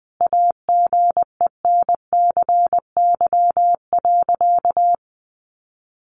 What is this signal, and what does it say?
Morse code, heard as a keyed tone, about 20 wpm, 700 Hz. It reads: AGENCY.
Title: Sentences from Top 1300 Words 20wpm